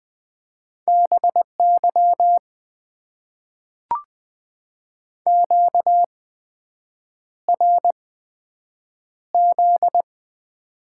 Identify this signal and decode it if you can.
Morse code, heard as a keyed tone, about 20 wpm, spaced as for 7 wpm, 700 Hz. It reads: BY QRZ
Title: Top 90 QSO Elements - 8x Character Spacing - 20wpm